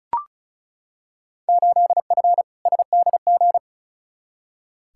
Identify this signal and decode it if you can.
Morse code, heard as a keyed tone, about 36 wpm, 700 Hz. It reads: 8F SDG
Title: Field Day Exchanges Encore - 35wpm